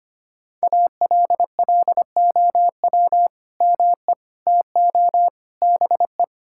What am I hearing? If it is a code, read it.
Morse code, heard as a keyed tone, about 25 wpm, 700 Hz. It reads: ALLOW ME TO BE
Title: Sentences from Top 1000 Words - Review Part 5 - 25wpm